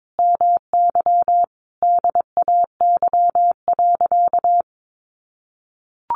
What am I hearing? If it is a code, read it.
Morse code, heard as a keyed tone, about 22 wpm, 700 Hz. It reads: MY DAY.
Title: Sentences from Top 1100 Words 22wpm